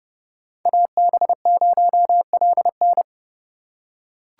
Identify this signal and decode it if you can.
Morse code, heard as a keyed tone, about 30 wpm, 700 Hz. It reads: AB0LN
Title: USA Call Signs - 8x Character Spacing - 30wpm